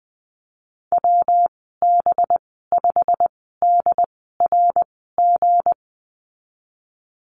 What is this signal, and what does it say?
Morse code, heard as a keyed tone, about 20 wpm, spaced as for 14 wpm, 700 Hz. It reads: WB5DRG
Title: 231 Lesson 37g - Letter x - Call Signs 20wpm